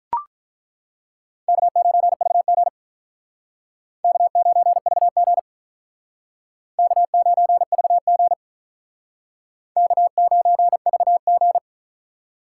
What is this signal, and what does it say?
Morse code, heard as a keyed tone, about 43 wpm, 700 Hz. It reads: K9VG K9VG K9VG K9VG
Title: Speed Racing - USA 1x2 Call Signs 35wpm